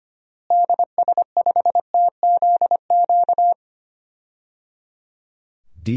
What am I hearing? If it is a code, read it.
Morse code, heard as a keyed tone, about 25 wpm, 700 Hz. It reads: DS5TZQ
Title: DXCC-100 Call Signs - 6 Characters - Repeated 3x - 25wpm